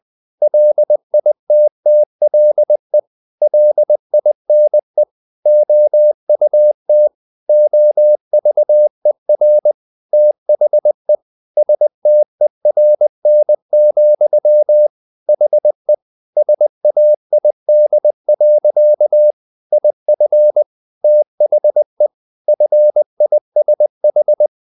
Morse code, heard as a keyed tone, about 20 wpm, 600 Hz. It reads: LITTLE LINE OUT OVER THE STERN, HE SAID. IF THE FISH
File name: the old man and the sea / the old man and the sea part 1 20wpm